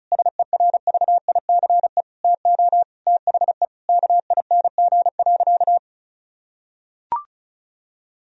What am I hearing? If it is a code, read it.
Morse code, heard as a keyed tone, about 35 wpm, 700 Hz. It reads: SERVICE TO THE KING.
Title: Sentences from Top 1000 Words - Review Part 4 - 35wpm